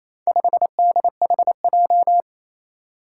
Morse code, heard as a keyed tone, about 28 wpm, 700 Hz. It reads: 5DHJ
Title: Four Challenging Letter-Number Combinations 28wpm